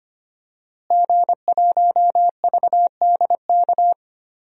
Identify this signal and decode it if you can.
Morse code, heard as a keyed tone, about 25 wpm, 700 Hz. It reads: G1VDK
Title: DXCC-100 Call Signs - 5 Characters - Repeated 3x - 25wpm